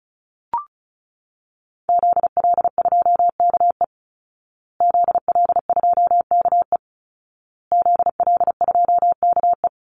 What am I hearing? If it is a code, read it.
Morse code, heard as a keyed tone, about 34 wpm, 700 Hz. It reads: ZL2KE ZL2KE ZL2KE
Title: Call Signs Repeated 2x - 35wpm